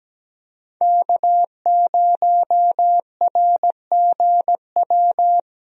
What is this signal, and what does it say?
Morse code, heard as a keyed tone, about 17 wpm, 700 Hz. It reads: K0RGW
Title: USA Call Signs - 5 Characters - 17wpm